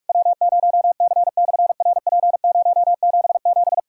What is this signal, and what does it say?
Morse code, heard as a keyed tone, about 46 wpm, 700 Hz. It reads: W0C/RP077
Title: SOTA QSO Practice 45wpm